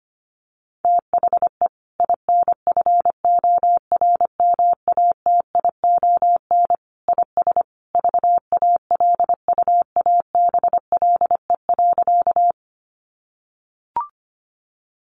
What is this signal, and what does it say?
Morse code, heard as a keyed tone, about 25 wpm, 700 Hz. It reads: THE INFORMATION IS VALUABLE.
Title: Sentences from Top 1100 Words 25wpm